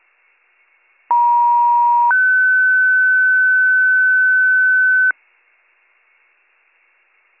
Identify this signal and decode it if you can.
Two-tone page, heard as a one-second tone, then a three-second tone, about 950 then 1500 Hz. It is heard from 1.1 to 5.1 s.